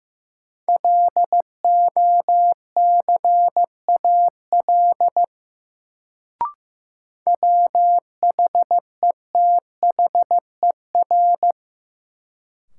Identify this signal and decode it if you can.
Morse code, heard as a keyed tone, about 15 wpm, 700 Hz. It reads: LOCAL WHETHER
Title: Sentences from Top 1000 Words - Review Part 3 - New Word Drill - 15wpm